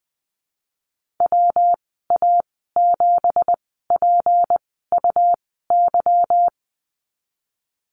Morse code, heard as a keyed tone, about 19 wpm, spaced as for 14 wpm, 700 Hz. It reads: WA7PUY
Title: 213 Lesson 34g - Number 8 - Call Signs 20wpm